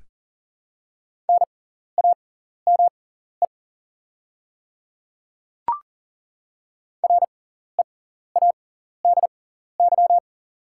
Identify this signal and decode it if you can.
Morse code, heard as a keyed tone, about 38 wpm, spaced as for 14 wpm, 690 Hz. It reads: NAME READY
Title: 6x Character Spacing - Top 500 Words - 40wpm